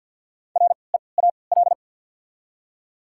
Morse code, heard as a keyed tone, about 54 wpm, spaced as for 29 wpm, 700 Hz. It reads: REAL
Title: 3x Character Spacing - Top 300 Words - 50wpm